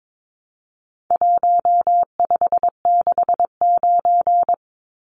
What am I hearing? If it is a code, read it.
Morse code, heard as a keyed tone, about 22 wpm, 700 Hz. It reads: 1569
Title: Numbers - Contest Style - 4-Digit 22wpm